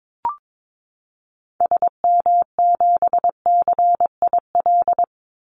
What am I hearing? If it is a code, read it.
Morse code, heard as a keyed tone, about 22 wpm, 700 Hz. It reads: SM7CIL
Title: Call Signs Repeated 2x - 22wpm